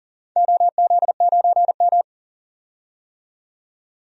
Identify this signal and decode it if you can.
Morse code, heard as a keyed tone, about 40 wpm, 700 Hz. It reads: OZ9M
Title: DXCC-100 Call Signs - 4 Characters - 40wpm